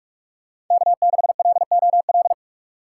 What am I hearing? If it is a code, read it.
Morse code, heard as a keyed tone, about 46 wpm, 700 Hz. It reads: K6LOL
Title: USA Call Signs - 2x Character Spacing - 45wpm